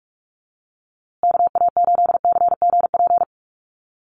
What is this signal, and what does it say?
Morse code, heard as a keyed tone, about 44 wpm, 700 Hz. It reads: KA8CGP
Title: USA Call Signs Repeated 2x 45wpm